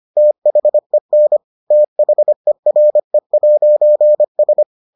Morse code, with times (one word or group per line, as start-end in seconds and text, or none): 0.2-1.4 THEN
1.7-4.6 THERE'S